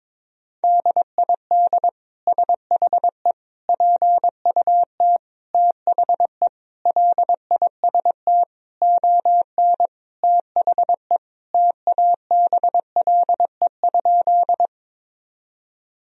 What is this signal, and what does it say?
Morse code, heard as a keyed tone, about 22 wpm, 700 Hz. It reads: DID SHE PUT THE LIST ON THE TABLE?
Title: Sentences from Top 1000 Words - Review Part 2 - 22wpm